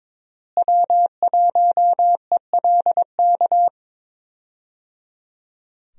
Morse code, heard as a keyed tone, about 22 wpm, 700 Hz. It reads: W1ELK
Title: USA Call Signs - 5 Characters - Repeated 3x - 22wpm